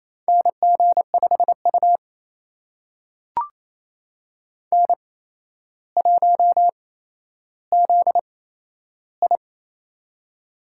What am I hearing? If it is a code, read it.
Morse code, heard as a keyed tone, about 28 wpm, spaced as for 12 wpm, 700 Hz. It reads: NG5U N1ZI
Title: USA Call Signs - 4 Characters - 8x Character Spacing - 28wpm